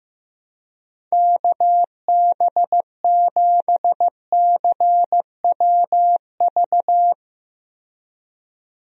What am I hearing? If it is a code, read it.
Morse code, heard as a keyed tone, about 15 wpm, 700 Hz. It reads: KB7CWV
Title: USA Call Signs - 6 Characters - 15wpm